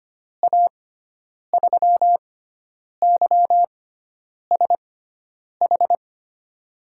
Morse code, heard as a keyed tone, about 25 wpm, spaced as for 9 wpm, 700 Hz. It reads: A3YSH
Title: USA Call Signs - 6 Characters - 6x Character Spacing - 25wpm